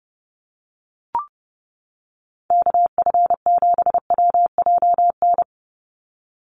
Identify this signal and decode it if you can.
Morse code, heard as a keyed tone, about 30 wpm, 700 Hz. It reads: KF7WJN
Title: USA Call Signs 30wpm